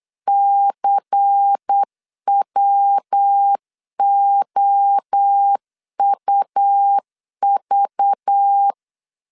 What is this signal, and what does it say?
Morse code, heard as a keyed tone, about 8 wpm, 800 Hz. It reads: CWOUV
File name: 8wpm